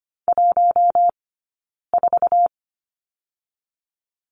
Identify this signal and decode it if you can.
Morse code, heard as a keyed tone, about 25 wpm, 700 Hz. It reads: 1 4
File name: Lesson 15d - Number 5 - Two Characters 25wpm